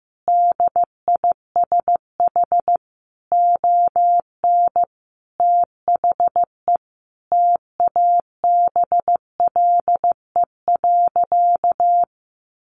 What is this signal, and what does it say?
Morse code, heard as a keyed tone, about 14 wpm, 700 Hz. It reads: DISH ON THE TABLE.
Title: Sentences from Top 1300 Words 15wpm